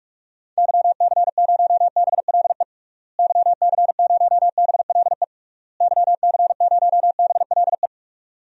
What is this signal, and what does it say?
Morse code, heard as a keyed tone, about 46 wpm, 700 Hz. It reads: YC0BLE YC0BLE YC0BLE
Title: DXCC-100 Call Signs - 6 Characters - Repeated 3x - 45wpm